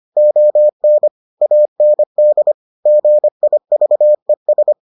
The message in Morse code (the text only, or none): ON AND GIVES